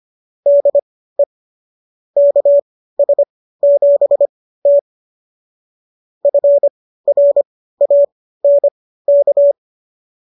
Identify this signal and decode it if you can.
Morse code, heard as a keyed tone, about 25 wpm, spaced as for 15 wpm, 570 Hz. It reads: DE KS7T FRANK